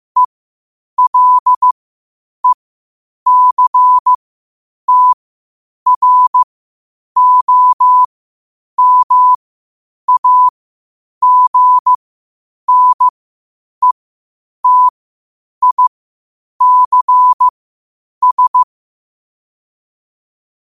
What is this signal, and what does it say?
Morse code, heard as a keyed tone, about 15 wpm, spaced as for 8 wpm, 1000 Hz. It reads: ELECTROMAGNETICS
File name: qotdfiles / 8wpm